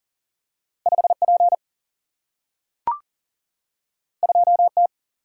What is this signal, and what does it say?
Morse code, heard as a keyed tone, about 38 wpm, 700 Hz. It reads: 5 P 2 T
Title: Single-Letter Single-Number 40wpm